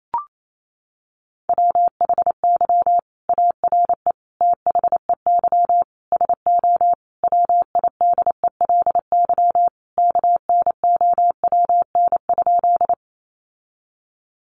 Morse code, heard as a keyed tone, about 28 wpm, 700 Hz. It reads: WHY ARE THEY SO WIDELY KNOWN?
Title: Sentences from Top 1200 Words 28wpm